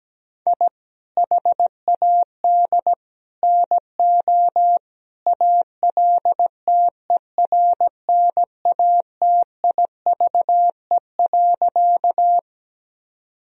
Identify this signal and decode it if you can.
Morse code, heard as a keyed tone, about 17 wpm, 700 Hz. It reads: I HAD NO ALTERNATIVE.
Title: Sentences from Top 1000 Words - Review Part 4 - 17wpm